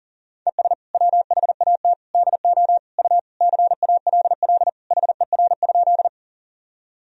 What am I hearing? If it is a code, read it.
Morse code, heard as a keyed tone, about 40 wpm, 700 Hz. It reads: ES WHAT DO U CALL HER?
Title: Sentences from Top 100 Words and Abbreviations 40wpm